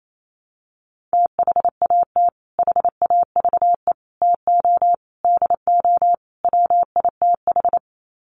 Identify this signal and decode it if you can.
Morse code, heard as a keyed tone, about 28 wpm, 700 Hz. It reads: THAT HAVE TO DO WITH
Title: Sets of 5 Words Encore2 28wpm